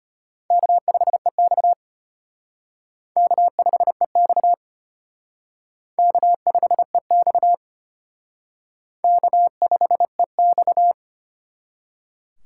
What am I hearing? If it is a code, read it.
Morse code, heard as a keyed tone, about 31 wpm, 700 Hz. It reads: K5EX K5EX K5EX K5EX